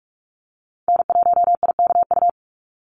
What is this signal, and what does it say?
Morse code, heard as a keyed tone, about 44 wpm, 700 Hz. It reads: N1IKU
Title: USA Call Signs 45wpm